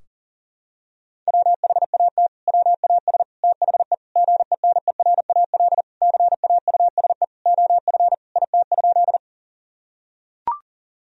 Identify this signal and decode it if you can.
Morse code, heard as a keyed tone, about 40 wpm, 700 Hz. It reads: WHAT WAS THE GENERAL CAUSE OF IT?
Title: Sentences from Top 1000 Words - Review Part 3 - 40wpm